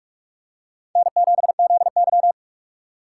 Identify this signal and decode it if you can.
Morse code, heard as a keyed tone, about 44 wpm, 690 Hz. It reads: N7ZY
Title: USA Call Signs - 4 Characters - 2x Character Spacing - 45wpm